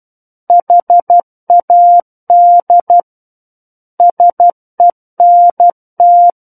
Morse code, heard as a keyed tone, about 12 wpm, 700 Hz. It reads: HAD SENT
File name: Book One / 03-Chapter 3 - 12 WPM